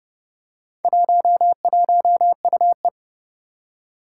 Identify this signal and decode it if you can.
Morse code, heard as a keyed tone, about 30 wpm, 700 Hz. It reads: 11UE
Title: Four Challenging Letter-Number Combinations 30wpm